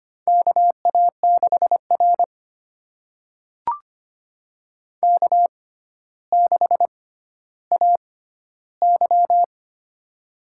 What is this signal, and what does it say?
Morse code, heard as a keyed tone, about 25 wpm, spaced as for 13 wpm, 700 Hz. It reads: KA6R K6AY
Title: USA Call Signs - 4 Characters - 6x Character Spacing - 25wpm